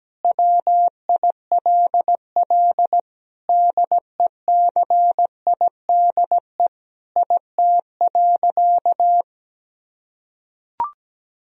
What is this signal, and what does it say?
Morse code, heard as a keyed tone, about 17 wpm, 700 Hz. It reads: WILL DECIDE IT.
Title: Sentences from Top 1000 Words - Review Part 3 - 17wpm